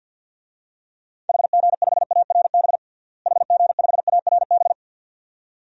Morse code, heard as a keyed tone, about 49 wpm, 700 Hz. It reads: HG5ARB HG5ARB
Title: DXCC-100 Call Signs - 6 Characters - Repeated 2x - 50wpm